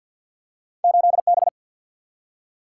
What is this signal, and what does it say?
Morse code, heard as a keyed tone, about 53 wpm, 700 Hz. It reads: 8 B
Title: Single-Letter Single-Number 50wpm